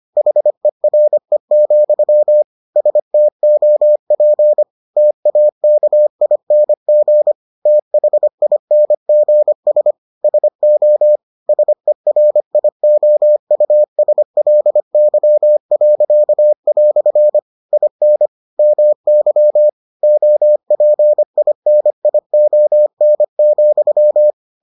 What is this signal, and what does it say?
Morse code, heard as a keyed tone, about 25 wpm, 600 Hz. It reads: HERE, STOP TAKING THINGS SO SERIOUSLY." IN MY OPINION,